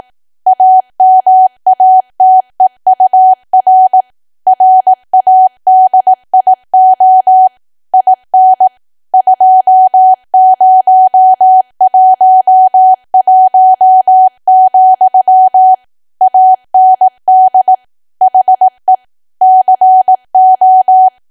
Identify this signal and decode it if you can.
Morse code, heard as a keyed tone, about 18 wpm, 750 Hz. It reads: AMATEUR RADIO IN 2011, AND HE CO
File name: Archive / 18 WPM / 18WPM